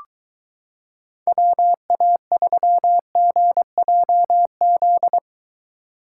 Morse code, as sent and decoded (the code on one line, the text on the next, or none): .-- .- ...-- --. .--- --..
WA3GJZ